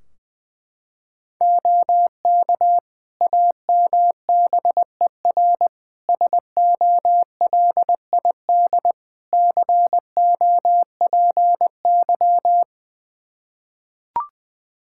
Morse code, heard as a keyed tone, about 20 wpm, 700 Hz. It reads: OK AMBER SOLID COPY
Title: Practice QSO - Exchange 3 - More Info - 20wpm